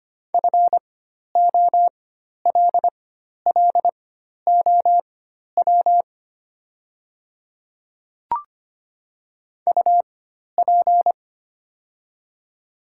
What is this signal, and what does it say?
Morse code, heard as a keyed tone, about 25 wpm, spaced as for 12 wpm, 700 Hz. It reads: FOLLOW UP
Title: 4x Character Spacing - Top 200 Words - 25wpm